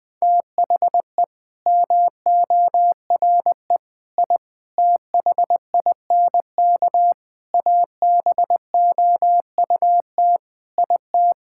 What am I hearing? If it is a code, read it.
Morse code, heard as a keyed tone, about 20 wpm, 700 Hz. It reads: THE MORE I THINK ABOUT IT
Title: Sets of 8 Words - 20wpm